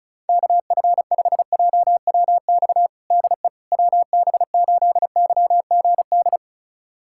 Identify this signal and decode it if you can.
Morse code, heard as a keyed tone, about 36 wpm, 700 Hz. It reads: KF5JWX DE WB8YGD
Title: USA Callsign DE Callsign 35wpm